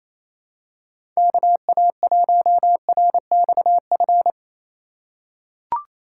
Morse code, heard as a keyed tone, about 28 wpm, 700 Hz. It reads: KA1RXF
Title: USA Call Signs - 6 Characters - 3x Character Spacing - 28wpm